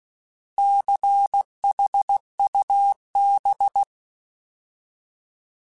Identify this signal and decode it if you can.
Morse code, heard as a keyed tone, about 16 wpm, 800 Hz. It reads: CHUB
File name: Alphabetical / Alphabetical mp3s / Cs @15